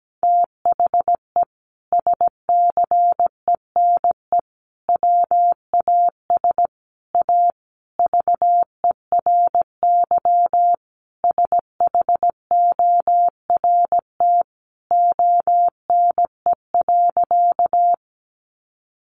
Morse code, heard as a keyed tone, about 17 wpm, 700 Hz. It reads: THE SCENE WAS A VERY SHORT ONE.
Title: Sentences from Top 1100 Words 17wpm